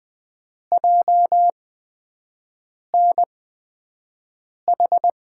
Morse code, heard as a keyed tone, about 20 wpm, spaced as for 5 wpm, 700 Hz. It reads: JNH
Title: USA Call Signs - 6 Characters - 8x Character Spacing - 20wpm